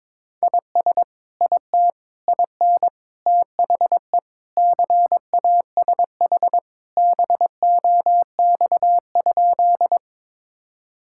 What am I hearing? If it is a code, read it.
Morse code, heard as a keyed tone, about 22 wpm, 700 Hz. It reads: IS IT IN THE CASH BOX?
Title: Sentences from Top 1000 Words - Review Part 5 - 22wpm